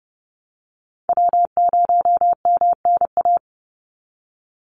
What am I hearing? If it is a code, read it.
Morse code, heard as a keyed tone, about 30 wpm, 700 Hz. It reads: W0MNA
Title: NA SOTA Chaser Callsigns 30wpm